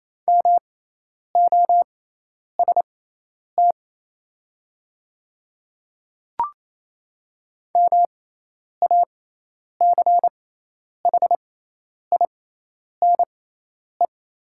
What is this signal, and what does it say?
Morse code, heard as a keyed tone, about 28 wpm, spaced as for 10 wpm, 700 Hz. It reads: MOST MACHINE
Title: 6x Character Spacing - Top 500 Words - 28wpm